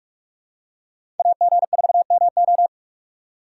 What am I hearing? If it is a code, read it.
Morse code, heard as a keyed tone, about 47 wpm, 700 Hz. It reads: AG4MO
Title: USA Call Signs - 5 Characters - 2x Character Spacing - 45wpm